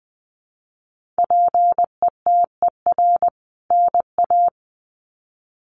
Morse code, heard as a keyed tone, about 20 wpm, 700 Hz. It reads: PETER NA
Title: CWT Exchange 20wpm